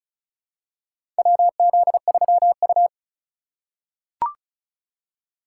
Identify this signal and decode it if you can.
Morse code, heard as a keyed tone, about 36 wpm, 700 Hz. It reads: WZ3U
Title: USA Call Signs - 4 Characters - 4x Character Spacing - 35wpm